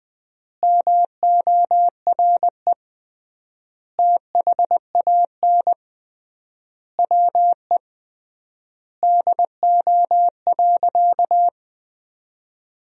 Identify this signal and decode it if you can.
Morse code, heard as a keyed tone, about 20 wpm, 700 Hz. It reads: MORE THAN WE DO.
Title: 3x Word-Spacing - Sentences from Top 100 Words - 20wpm